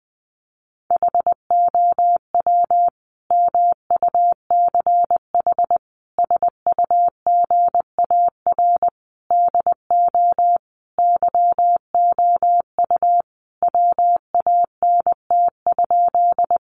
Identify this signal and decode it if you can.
Morse code, heard as a keyed tone, about 20 wpm, 700 Hz. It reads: HOW MUCH SUGAR DO YOU WANT?
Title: Sentences from Top 1200 Words 20wpm